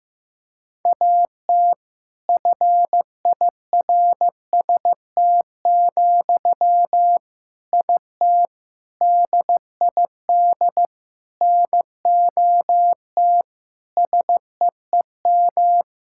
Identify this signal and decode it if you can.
Morse code, heard as a keyed tone, about 15 wpm, 700 Hz. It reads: AT FIRST, IT DID NOT SEEM
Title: Sentences from Top 1000 Words - Review Part 6 - 15wpm